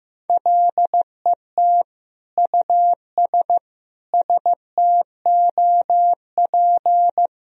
Morse code, heard as a keyed tone, about 15 wpm, 700 Hz. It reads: LET US STOP
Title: Sentences from Top 1000 Words - Review Part 1 - 15wpm